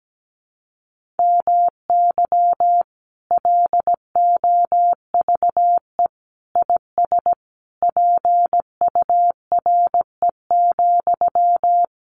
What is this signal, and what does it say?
Morse code, heard as a keyed tone, about 17 wpm, 700 Hz. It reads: MY LOVE IS PURE,
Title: Sentences from Top 1500 Words 17wpm